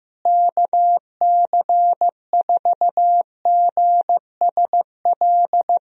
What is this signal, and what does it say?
Morse code, heard as a keyed tone, about 15 wpm, 700 Hz. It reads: KC4GSL
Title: USA Call Signs - 6 Characters - Repeated 2x - 15wpm